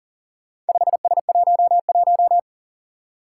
Morse code, heard as a keyed tone, about 43 wpm, 700 Hz. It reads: 5S11
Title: Four Challenging Letter-Number Combinations 40wpm